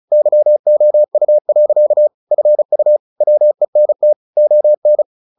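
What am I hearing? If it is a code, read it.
Morse code, heard as a keyed tone, about 34 wpm, 600 Hz. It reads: YOU. FU WENT ON
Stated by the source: Morse practice at 35 wpm from CW med Gnister, sending Ken Follett's novel Never in English.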